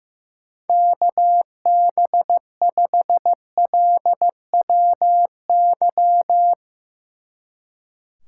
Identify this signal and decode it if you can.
Morse code, heard as a keyed tone, about 18 wpm, 700 Hz. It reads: KB5LWY
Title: Speed Racing - USA Call Signs 15wpm